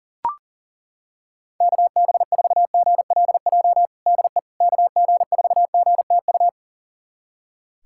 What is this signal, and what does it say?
Morse code, heard as a keyed tone, about 40 wpm, 700 Hz. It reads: KB4GLJ DE KG4GTU
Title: USA Callsign DE Callsign - 6 Characters - 40wpm